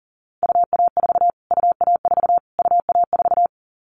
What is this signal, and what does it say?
Morse code, heard as a keyed tone, about 40 wpm, 700 Hz. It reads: UA4 UA4 UA4
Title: Callsign Prefixes for Top 25 DXCC - Repeated 3x - 40wpm